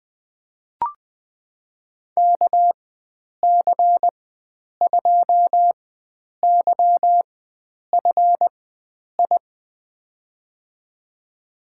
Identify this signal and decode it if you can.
Morse code, heard as a keyed tone, about 20 wpm, spaced as for 9 wpm, 700 Hz. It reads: KC2YFI
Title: USA Call Signs - 4x Character Spacing - 20wpm